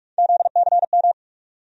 Morse code, heard as a keyed tone, about 45 wpm, 700 Hz. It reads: ZCM
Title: USA Call Sign Postfixes - 3 Letters 45wpm